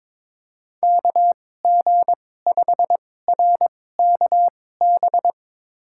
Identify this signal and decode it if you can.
Morse code, heard as a keyed tone, about 22 wpm, spaced as for 16 wpm, 700 Hz. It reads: KG5RKB
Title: USA Call Signs - 6 Characters - 2x Character Spacing - 22wpm